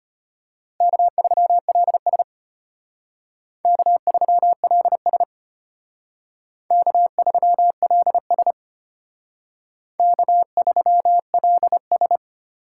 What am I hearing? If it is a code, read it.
Morse code, heard as a keyed tone, about 33 wpm, 700 Hz. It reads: K3LS K3LS K3LS K3LS